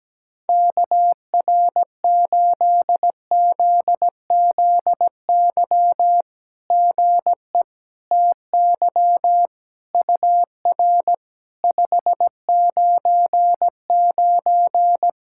Morse code, heard as a keyed tone, about 17 wpm, 700 Hz. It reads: KR8ZZY GE TY UR 599
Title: SOTA QSO Practice 17wpm